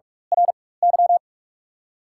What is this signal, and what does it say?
Morse code, heard as a keyed tone, about 48 wpm, spaced as for 21 wpm, 690 Hz. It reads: RY
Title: 4x Character Spacing - Top 300 Words - 45wpm